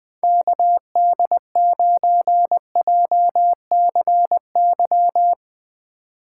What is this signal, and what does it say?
Morse code, heard as a keyed tone, about 19 wpm, 700 Hz. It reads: KD9JCY
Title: Speed Racing - USA 2x3 Call Signs 17wpm